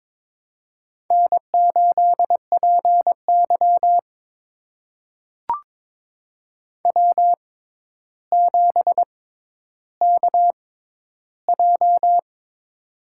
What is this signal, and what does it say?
Morse code, heard as a keyed tone, about 21 wpm, spaced as for 11 wpm, 700 Hz. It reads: N8PY W7KJ
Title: USA Call Signs - 4 Characters - 6x Character Spacing - 22wpm